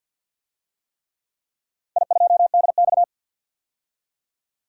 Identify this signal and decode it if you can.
Morse code, heard as a keyed tone, about 51 wpm, 700 Hz. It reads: I2DX
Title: DXCC-100 Call Signs - 4 Characters - 50wpm